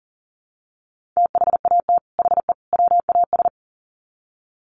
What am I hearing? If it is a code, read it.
Morse code, heard as a keyed tone, about 40 wpm, 700 Hz. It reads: THAT HE WAS
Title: Sets of 3 Words 40wpm